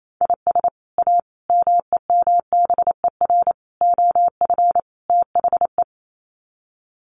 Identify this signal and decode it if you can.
Morse code, heard as a keyed tone, about 28 wpm, 700 Hz. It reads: IS A MEMBER OF THE
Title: Sets of 5 Words 28wpm